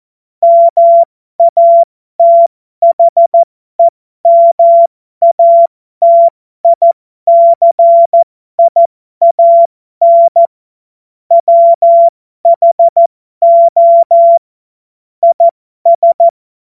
Morse code, heard as a keyed tone, about 14 wpm, spaced as for 12 wpm, 680 Hz. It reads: MATHEMATICIAN WHO IS
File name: Morsekurs / Witze